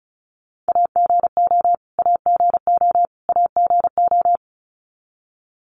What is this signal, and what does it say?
Morse code, heard as a keyed tone, about 34 wpm, 700 Hz. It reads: AGO AGO AGO
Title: Top 500 Words Repeated 35wpm